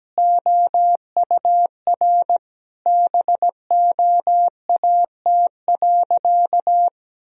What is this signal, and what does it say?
Morse code, heard as a keyed tone, about 17 wpm, 700 Hz. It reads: OUR BOAT.
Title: Sentences from Top 1000 Words - Review Part 4 - 17wpm